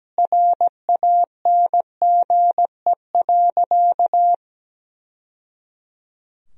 Morse code, heard as a keyed tone, about 17 wpm, 700 Hz. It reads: RANGE.
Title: Sentences from Top 1000 Words - Review Part 4 - 17wpm